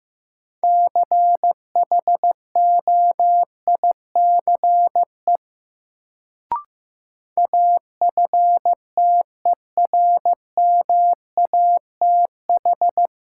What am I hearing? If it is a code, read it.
Morse code, heard as a keyed tone, about 15 wpm, 700 Hz. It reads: CHOICE AFTERMATH
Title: Sets of 5 Words - New Word Drill - 15wpm